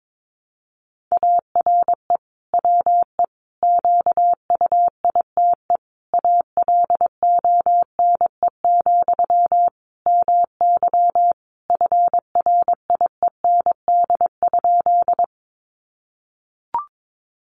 Morse code, heard as a keyed tone, about 22 wpm, 700 Hz. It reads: ARE WE QUITE ALONE, MY FRIEND?
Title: Sentences from Top 1000 Words 22wpm